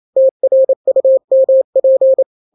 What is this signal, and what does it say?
Morse code, heard as a keyed tone, about 28 wpm, 530 Hz. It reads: TRUMP
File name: daily / 27wpm